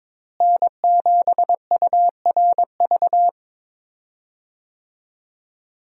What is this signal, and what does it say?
Morse code, heard as a keyed tone, about 22 wpm, 700 Hz. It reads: N7URV
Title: USA Call Signs - 5 Characters - Repeated 3x - 22wpm